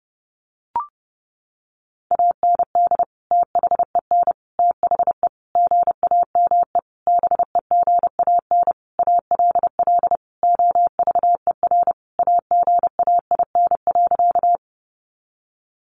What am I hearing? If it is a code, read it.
Morse code, heard as a keyed tone, about 28 wpm, 700 Hz. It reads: AND THEN THE GAME BEGAN ALL OVER AGAIN.
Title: Sentences from Top 500 Words 30wpm